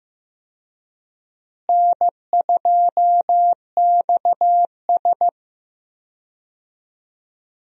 Morse code, heard as a keyed tone, about 15 wpm, 700 Hz. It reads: N2XS
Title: USA Call Signs - 4 Characters - Repeated 2x - 15wpm